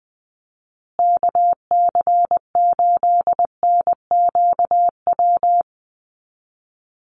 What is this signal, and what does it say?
Morse code, heard as a keyed tone, about 20 wpm, 700 Hz. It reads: KC8NQW
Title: USA Call Signs Repeated 3x 20wpm